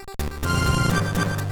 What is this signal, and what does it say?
There is no code.